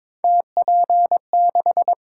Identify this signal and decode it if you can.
Morse code, heard as a keyed tone, about 22 wpm, 700 Hz. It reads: TP6
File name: practice / 22wpm / Three Letter-Number Combinations - 22wpm